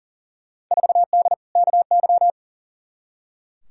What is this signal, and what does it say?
Morse code, heard as a keyed tone, about 40 wpm, 700 Hz. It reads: 4D KY
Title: Field Day Exchanges Encore - 40wpm